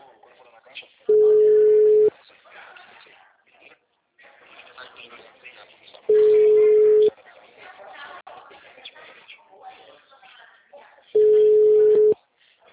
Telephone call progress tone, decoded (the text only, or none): ringback tone